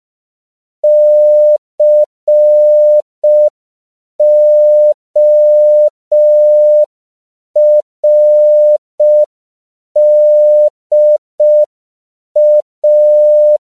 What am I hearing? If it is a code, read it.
Morse code, heard as a keyed tone, about 5 wpm, 600 Hz. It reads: CORDA
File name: CW / 5PPM